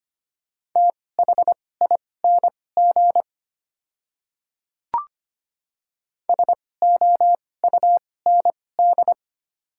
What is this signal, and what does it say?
Morse code, heard as a keyed tone, about 25 wpm, spaced as for 18 wpm, 690 Hz. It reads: THING SOUND
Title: 2x Character Spacing - Top 100 Words - 25wpm